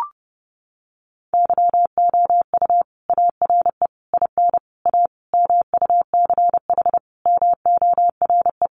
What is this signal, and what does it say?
Morse code, heard as a keyed tone, about 30 wpm, 700 Hz. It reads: YOU ARE IN A MUCH MORE
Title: Sentences from Top 1400 Words 30wpm